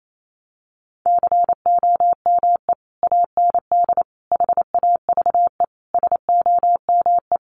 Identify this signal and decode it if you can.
Morse code, heard as a keyed tone, about 28 wpm, 700 Hz. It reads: COME AND HAVE SOME
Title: Sentences from Top 1500 Words 28wpm